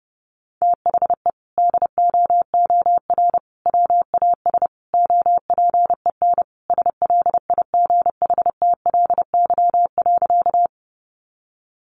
Morse code, heard as a keyed tone, about 30 wpm, 700 Hz. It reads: THE DOOR WAS OPEN SLIGHTLY.
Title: Sentences from Top 1300 Words 30wpm